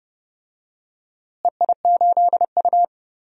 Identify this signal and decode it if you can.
Morse code, heard as a keyed tone, about 31 wpm, 700 Hz. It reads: EI8U
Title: DXCC-100 Call Signs - 4 Characters - Repeated 2x - 30wpm